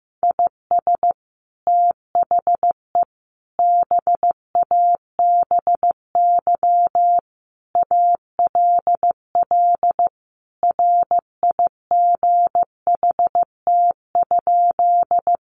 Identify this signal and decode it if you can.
Morse code, heard as a keyed tone, about 15 wpm, 700 Hz. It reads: IS THE BABY ALL RIGHT?
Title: Sentences from Top 1300 Words 15wpm